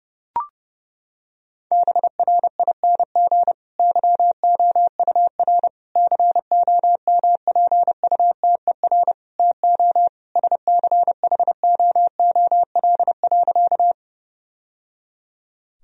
Morse code, heard as a keyed tone, about 30 wpm, 700 Hz. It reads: BRING YOUR COMPUTER TO SCHOOL.
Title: Sentences from Top 1000 Words - Review Part 3 - 30wpm